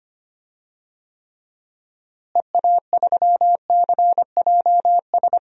Morse code, heard as a keyed tone, about 25 wpm, 700 Hz. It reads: EA3CJS
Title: DXCC-100 Call Signs - 6 Characters - Repeated 2x - 25wpm